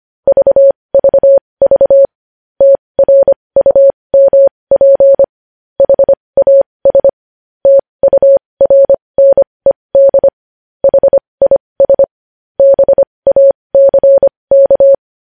Morse code, heard as a keyed tone, about 25 wpm, spaced as for 20 wpm, 550 Hz. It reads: VVV TRUMP HAS TURNED HIS BACK
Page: news headlines in morse code at 20 wpm